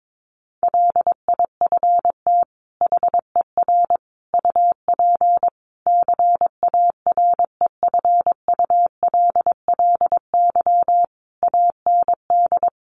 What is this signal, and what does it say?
Morse code, heard as a keyed tone, about 22 wpm, 700 Hz. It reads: LIFT HER UP CAREFULLY AND
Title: Sentences from Top 1300 Words 22wpm